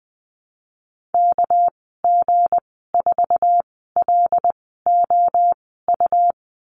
Morse code, heard as a keyed tone, about 20 wpm, spaced as for 15 wpm, 700 Hz. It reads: KG4LOU
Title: 219 Lesson 35g - Number 0 - Call Signs 20wpm